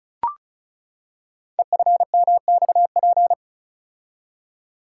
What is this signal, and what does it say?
Morse code, heard as a keyed tone, about 36 wpm, 690 Hz. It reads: EFMXP
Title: Code Groups - 5 Letters - 35wpm